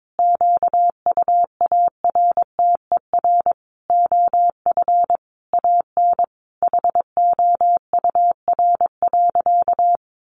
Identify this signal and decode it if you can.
Morse code, heard as a keyed tone, about 22 wpm, 700 Hz. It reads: QUARTER OF AN HOUR.
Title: Sentences from Top 1300 Words 22wpm